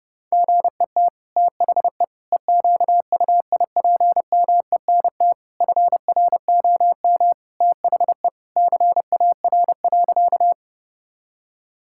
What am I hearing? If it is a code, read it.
Morse code, heard as a keyed tone, about 30 wpm, 700 Hz. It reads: GET THE EQUIPMENT FROM THE CAR.
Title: Sentences from Top 1000 Words - Review Part 5 - 30wpm